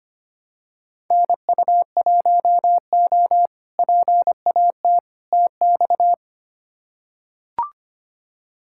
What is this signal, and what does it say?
Morse code, heard as a keyed tone, about 25 wpm, 700 Hz. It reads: NU1O PAT TX
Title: K1USN SST QSO Practice - 25wpm